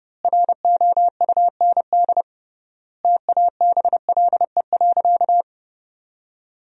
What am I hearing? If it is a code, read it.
Morse code, heard as a keyed tone, about 32 wpm, 700 Hz. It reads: ROUND TABLE.
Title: 3x Word-Spacing - Sentences from Top 400 Words - 30wpm